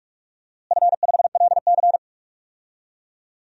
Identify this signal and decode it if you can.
Morse code, heard as a keyed tone, about 47 wpm, 700 Hz. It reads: F5LC
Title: DXCC-100 Call Signs - 4 Characters - Repeated 3x - 45wpm